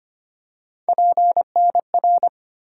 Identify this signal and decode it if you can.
Morse code, heard as a keyed tone, about 25 wpm, 700 Hz. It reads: PNR
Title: USA Call Sign Postfixes - 3 Letters 25wpm